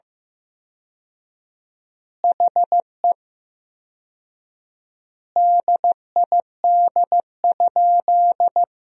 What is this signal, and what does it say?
Morse code, heard as a keyed tone, about 15 wpm, 700 Hz. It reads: HE DID?
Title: 4x Word-Spacing - Sentences from Top 100 Words - 15wpm